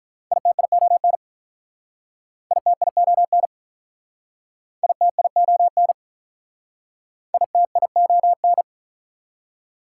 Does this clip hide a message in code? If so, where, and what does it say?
Yes: Morse code from 0.3 to 8.6 s, heard as a keyed tone, about 46 wpm, 700 Hz. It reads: ITION ITION ITION ITION